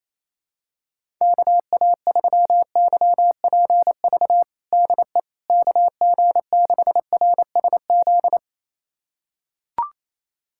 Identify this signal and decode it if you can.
Morse code, heard as a keyed tone, about 28 wpm, 700 Hz. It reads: KA3YPV DE KG6RSZ
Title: USA Callsign DE Callsign 28wpm